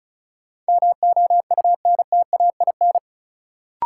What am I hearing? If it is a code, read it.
Morse code, heard as a keyed tone, about 36 wpm, 700 Hz. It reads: MOUNTAIN
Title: Sentences from Top 1000 Words - Review Part 1 - New Word Drill - 35wpm